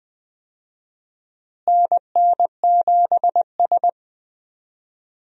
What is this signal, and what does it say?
Morse code, heard as a keyed tone, about 20 wpm, 700 Hz. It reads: NN7S
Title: USA Call Signs - 4 Characters - Repeated 2x - 20wpm